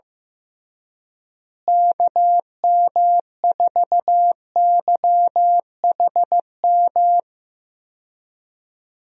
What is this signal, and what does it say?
Morse code, heard as a keyed tone, about 15 wpm, 700 Hz. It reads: KM4YHM
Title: USA Call Signs - 6 Characters - Repeated 3x - 15wpm